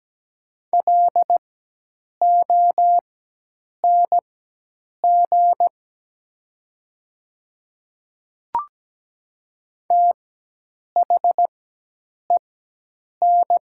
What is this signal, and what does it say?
Morse code, heard as a keyed tone, about 17 wpm, spaced as for 8 wpm, 690 Hz. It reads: LONG THEN
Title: 4x Character Spacing - Top 100 Words - 17wpm